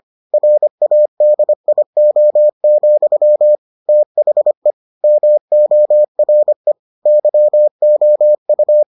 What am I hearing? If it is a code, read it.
Morse code, heard as a keyed tone, about 25 wpm, 600 Hz. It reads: RADIO, THE MORE YOU